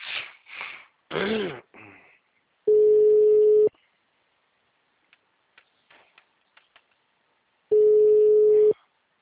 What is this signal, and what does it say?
Ringback tone (Europe, 425 Hz).
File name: out- -2033- - - .10541